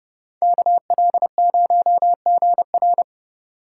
Morse code, heard as a keyed tone, about 29 wpm, 700 Hz. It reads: KL0GR